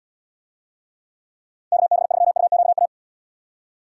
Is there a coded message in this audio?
Yes: Morse code from 1.7 to 2.9 s, heard as a keyed tone, about 77 wpm, 700 Hz. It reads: KB3UQA